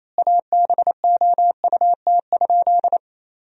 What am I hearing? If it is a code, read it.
Morse code, heard as a keyed tone, about 28 wpm, 700 Hz. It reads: ABOUT?
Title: 4x Word-Spacing - Sentences from Top 100 Words - 28wpm